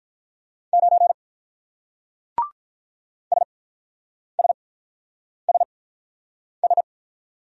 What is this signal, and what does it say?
Morse code, heard as a keyed tone, about 44 wpm, spaced as for 9 wpm, 700 Hz. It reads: 9 SSSS